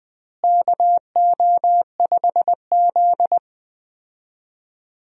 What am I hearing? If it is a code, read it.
Morse code, heard as a keyed tone, about 20 wpm, 700 Hz. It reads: KO5Z
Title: USA Call Signs - 4 Characters - Repeated 3x - 20wpm